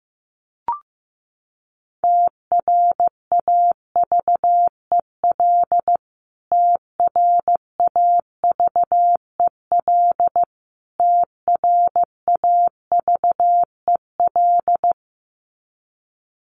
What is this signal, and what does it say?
Morse code, heard as a keyed tone, about 15 wpm, 700 Hz. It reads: TRAVEL TRAVEL TRAVEL
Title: Top 500 Words Repeated 15wpm